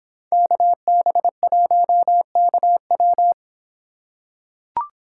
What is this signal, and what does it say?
Morse code, heard as a keyed tone, about 25 wpm, 700 Hz. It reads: KB1KW